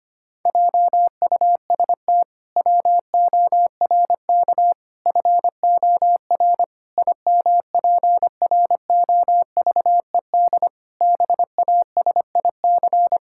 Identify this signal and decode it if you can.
Morse code, heard as a keyed tone, about 25 wpm, 700 Hz. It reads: JUST WORK FOR IMPROVED BASIC
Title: Pangram Challenge v2 - 25wpm